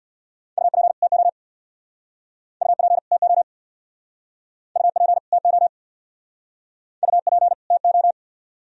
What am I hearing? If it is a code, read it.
Morse code, heard as a keyed tone, about 60 wpm, 700 Hz. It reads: UP TO UP TO UP TO UP TO
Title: Speed-Racing - Sets of 2 Words 50wpm